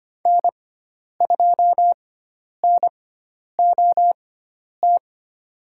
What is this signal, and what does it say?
Morse code, heard as a keyed tone, about 25 wpm, spaced as for 10 wpm, 700 Hz. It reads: N2NOT
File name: Lesson 16g - Letter c - Call Signs 25wpm